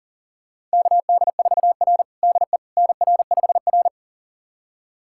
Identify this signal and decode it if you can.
Morse code, heard as a keyed tone, about 40 wpm, 700 Hz. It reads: KD4R DE NR5R
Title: USA Callsign DE Callsign - 4 Characters - 40wpm